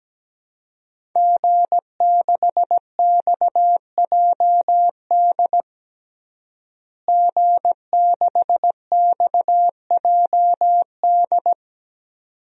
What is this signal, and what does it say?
Morse code, heard as a keyed tone, about 17 wpm, 700 Hz. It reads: G6XJD G6XJD
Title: DXCC-100 Call Signs - 5 Characters - Repeated 2x - 17wpm